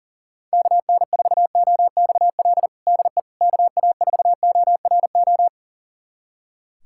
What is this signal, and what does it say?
Morse code, heard as a keyed tone, about 40 wpm, 700 Hz. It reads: KN4OXL DE KA4ORO
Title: USA Callsign DE Callsign 40wpm